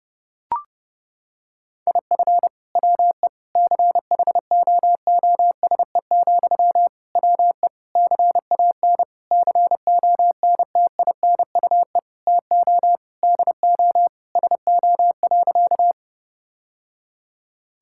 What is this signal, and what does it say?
Morse code, heard as a keyed tone, about 32 wpm, 700 Hz. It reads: IF WE CHOOSE, WE CAN CONTINUE TO DO SO.
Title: Sentences from Top 1000 Words - Review Part 6 - 30wpm